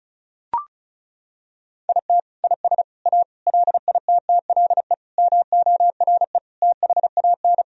Morse code, heard as a keyed tone, about 34 wpm, 700 Hz. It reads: IT IS A LITTLE MORE THAN
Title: Sentences from Top 1000 Words - Review Part 1 - 35wpm